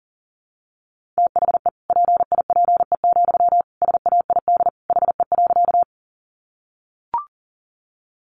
Morse code, heard as a keyed tone, about 40 wpm, 700 Hz. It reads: THE PIPE, SAID HE.
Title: Sentences from Top 1500 Words 40wpm